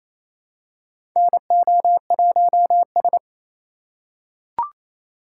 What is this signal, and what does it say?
Morse code, heard as a keyed tone, about 28 wpm, 700 Hz. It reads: NO1S
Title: USA Call Signs - 3x Character Spacing - 28wpm